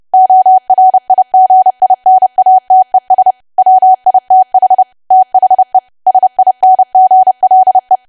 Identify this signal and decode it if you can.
Morse code, heard as a keyed tone, about 30 wpm, 750 Hz. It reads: ORIGINATES WITH THE SINGLE
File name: cwa / W1AW-Level3 / 30WPM